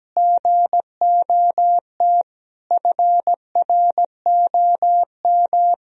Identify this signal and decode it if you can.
Morse code, heard as a keyed tone, about 17 wpm, 700 Hz. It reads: GOT FROM
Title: Sentences from Top 1000 Words - Review Part 2 - 17wpm